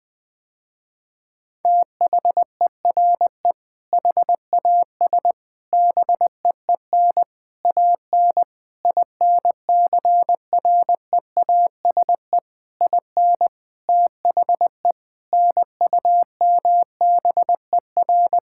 Morse code, heard as a keyed tone, about 20 wpm, 700 Hz. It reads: THERE HAS BEEN AN INCREASE IN THE NUMBER
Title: Sentences with 6-Word Phrases - Spoken Beginning - 20wpm